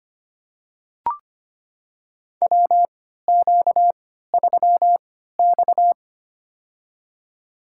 Morse code, heard as a keyed tone, about 25 wpm, spaced as for 14 wpm, 700 Hz. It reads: WQ3X
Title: USA Call Signs - 4 Characters - 3x Character Spacing - 25wpm